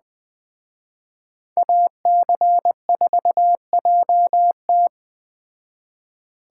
Morse code, heard as a keyed tone, about 20 wpm, 700 Hz. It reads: AC4JT